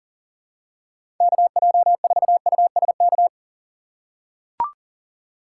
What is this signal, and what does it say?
Morse code, heard as a keyed tone, about 40 wpm, 700 Hz. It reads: KJ4USK
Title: USA Call Signs - 6 Characters - 4x Character Spacing - 40wpm